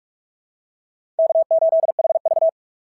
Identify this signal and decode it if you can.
Morse code, heard as a keyed tone, about 44 wpm, 660 Hz. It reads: K8HV
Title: SOTA QSO Practice 45wpm